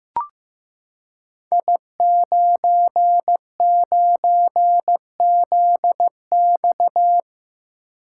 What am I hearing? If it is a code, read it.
Morse code, heard as a keyed tone, about 15 wpm, 700 Hz. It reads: I99ZX
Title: Code Groups - 5 Letter:Numbers - 15wpm